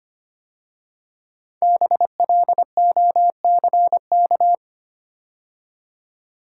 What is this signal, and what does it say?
Morse code, heard as a keyed tone, about 25 wpm, 700 Hz. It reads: BLOCK
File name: 5 Letter Words - First 3 Letters Given - 25wpm